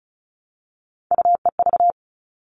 Morse code, heard as a keyed tone, about 34 wpm, 700 Hz. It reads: UEV
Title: All Cut Numbers - 3-Digit 35wpm